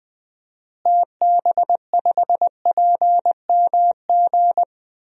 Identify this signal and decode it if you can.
Morse code, heard as a keyed tone, about 20 wpm, 700 Hz. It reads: TB5PMG
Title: DXCC-100 Call Signs - 6 Characters - Repeated 3x - 20wpm